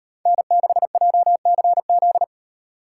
Morse code, heard as a keyed tone, about 35 wpm, 700 Hz. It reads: N6JCZ